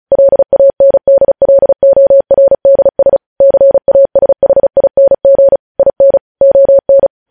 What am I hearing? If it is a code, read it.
Morse code, heard as a keyed tone, about 36 wpm, 550 Hz. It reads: LANDLORDS CASHING IN ON
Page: News headlines in Morse code at 35 wpm